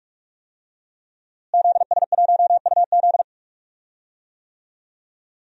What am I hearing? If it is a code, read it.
Morse code, heard as a keyed tone, about 46 wpm, 700 Hz. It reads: ZS1UZ
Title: DXCC-100 Call Signs - 5 Characters - Repeated 2x - 45wpm